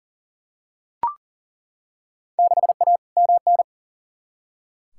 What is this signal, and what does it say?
Morse code, heard as a keyed tone, about 40 wpm, 700 Hz. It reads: 6A MN
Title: Field Day Exchanges Encore - 40wpm